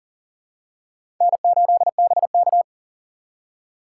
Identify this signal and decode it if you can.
Morse code, heard as a keyed tone, about 40 wpm, 700 Hz. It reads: N8BK
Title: USA Call Signs - 4 Characters - 6x Character Spacing - 40wpm